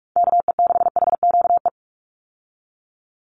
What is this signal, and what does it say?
Morse code, heard as a keyed tone, about 46 wpm, 700 Hz. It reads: KE6HQE
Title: USA Call Signs 45wpm